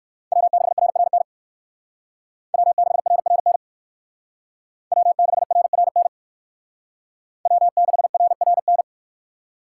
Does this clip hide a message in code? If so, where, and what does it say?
Yes: Morse code from 0.3 to 8.8 s, heard as a keyed tone, about 57 wpm, 700 Hz. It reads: W6RRN W6RRN W6RRN W6RRN